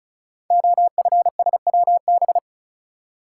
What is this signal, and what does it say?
Morse code, heard as a keyed tone, about 36 wpm, 700 Hz. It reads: OFSWB
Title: Code Groups - 5 Letters - 35wpm